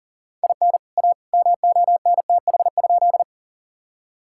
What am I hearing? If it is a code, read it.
Morse code, heard as a keyed tone, about 40 wpm, 700 Hz. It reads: IN A MONTH?
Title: Sentences from Top 1000 Words - Review Part 5 - 40wpm